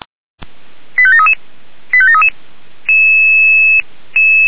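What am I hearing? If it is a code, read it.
Five-tone selective call (ZVEI), heard as quick runs of five tones, tones of about 70 ms. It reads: 85733 85733
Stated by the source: ZVEI five-tone call received via SDR.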